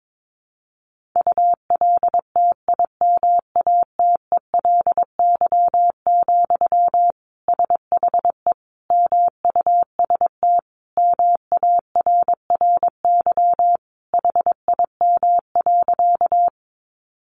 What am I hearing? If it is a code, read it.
Morse code, heard as a keyed tone, about 22 wpm, 700 Hz. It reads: ULTIMATELY, SHE MUST MARRY HIM.
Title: Sentences from Top 1200 Words 22wpm